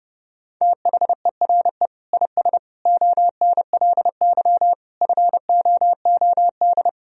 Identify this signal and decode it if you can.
Morse code, heard as a keyed tone, about 32 wpm, 700 Hz. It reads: THERE IS ONLY FOOD
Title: Sentences from Top 300 Words and Abbreviations 30wpm